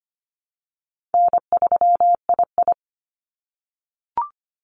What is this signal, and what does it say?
Morse code, heard as a keyed tone, about 25 wpm, 690 Hz. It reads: N3II
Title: NA SOTA Activator Callsigns 25wpm